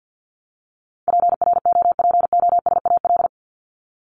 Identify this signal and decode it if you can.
Morse code, heard as a keyed tone, about 50 wpm, 700 Hz. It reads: PROPOSAL
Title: Top 1751-2000 Words 50wpm